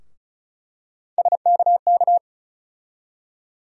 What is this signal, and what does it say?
Morse code, heard as a keyed tone, about 36 wpm, 700 Hz. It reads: SKK
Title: USA Call Sign Postfixes - 3 Letters 35wpm